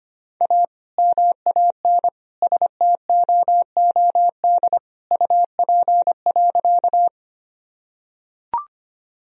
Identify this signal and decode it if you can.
Morse code, heard as a keyed tone, about 25 wpm, 700 Hz. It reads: A MAN STOOD UP.
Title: Sentences from Top 1000 Words - Review Part 2 - 25wpm